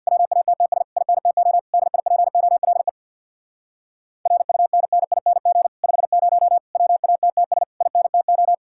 Morse code, heard as a keyed tone, about 63 wpm, 700 Hz. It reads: WATTS INTO DIPOLE RUNNING 50 WATTS INTO